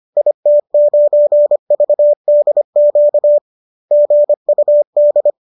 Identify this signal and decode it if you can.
Morse code, heard as a keyed tone, about 25 wpm, 590 Hz. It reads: IT9VDQ GUD